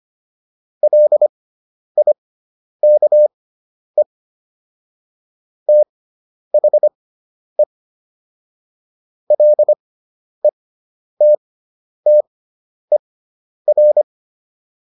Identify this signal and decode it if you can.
Morse code, heard as a keyed tone, about 25 wpm, spaced as for 10 wpm, 600 Hz. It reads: LIKE THE LETTER